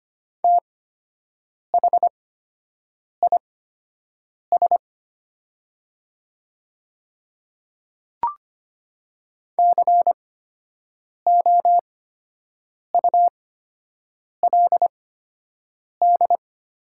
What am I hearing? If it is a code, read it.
Morse code, heard as a keyed tone, about 25 wpm, spaced as for 7 wpm, 700 Hz. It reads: THIS COULD
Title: 8x Character Spacing - Top 100 Words - 25wpm